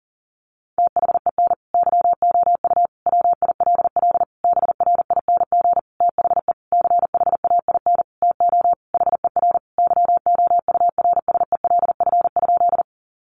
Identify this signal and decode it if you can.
Morse code, heard as a keyed tone, about 37 wpm, 700 Hz. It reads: THEN YOU WILL BRING THE CHAIN TO HER YOURSELF?
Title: Sentences from Top 1500 Words 40wpm